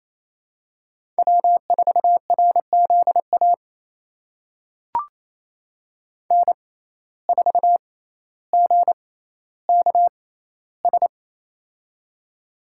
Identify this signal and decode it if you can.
Morse code, heard as a keyed tone, about 28 wpm, spaced as for 14 wpm, 700 Hz. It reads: W4RZA N4GKS
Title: USA Call Signs - 5 Characters - 6x Character Spacing - 28wpm